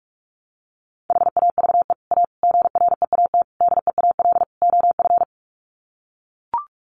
Morse code, heard as a keyed tone, about 44 wpm, 700 Hz. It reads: HAVE A GREAT DEAL OF
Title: Sets of 5 Words Encore2 45wpm